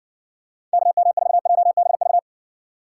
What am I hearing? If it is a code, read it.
Morse code, heard as a keyed tone, about 60 wpm, 700 Hz. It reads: KM4JBV